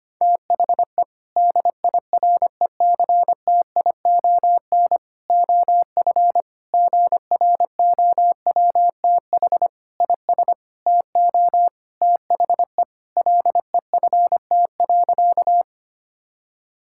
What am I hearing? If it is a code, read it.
Morse code, heard as a keyed tone, about 25 wpm, 700 Hz. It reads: THE DIRECTION OF GROWTH IS TO THE LEFT.
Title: Sentences from Top 1000 Words - Review Part 5 - 25wpm